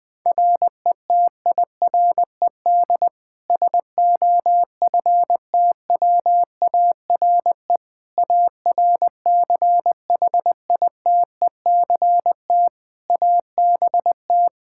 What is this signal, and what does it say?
Morse code, heard as a keyed tone, about 20 wpm, 700 Hz. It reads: RETIRED SOFTWARE ARCHITECT ABT
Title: Practice QSO - Exchange 3 - More Info - 20wpm